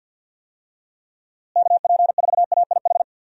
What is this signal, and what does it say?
Morse code, heard as a keyed tone, about 53 wpm, 700 Hz. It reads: KP4AIH
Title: DXCC-100 Call Signs - 6 Characters - Repeated 2x - 50wpm